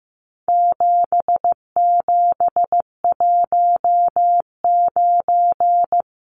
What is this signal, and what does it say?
Morse code, heard as a keyed tone, about 15 wpm, 700 Hz. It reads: 7719
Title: Numbers - 4-Digit 15wpm